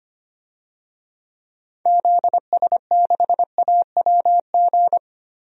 Morse code, heard as a keyed tone, about 25 wpm, 700 Hz. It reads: ZS6AWG
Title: DXCC-100 Call Signs - 6 Characters - 25wpm